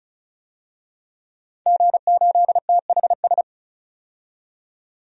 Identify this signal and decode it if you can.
Morse code, heard as a keyed tone, about 36 wpm, 700 Hz. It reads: G8THS
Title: DXCC-100 Call Signs - 5 Characters - 35wpm